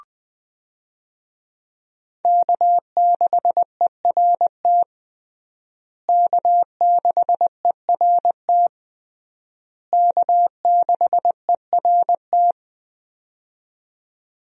Morse code, heard as a keyed tone, about 21 wpm, 700 Hz. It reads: K6ERT K6ERT K6ERT
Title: USA Call Signs - 5 Characters - Repeated 3x - 20wpm